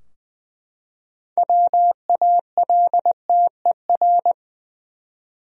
Morse code, heard as a keyed tone, about 21 wpm, 700 Hz. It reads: WALTER
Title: USA Common Names - 20wpm